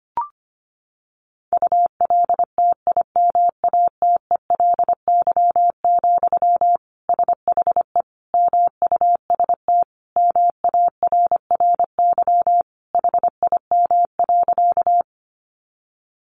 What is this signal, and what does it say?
Morse code, heard as a keyed tone, about 25 wpm, 700 Hz. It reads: ULTIMATELY, SHE MUST MARRY HIM.
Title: Sentences from Top 1200 Words 25wpm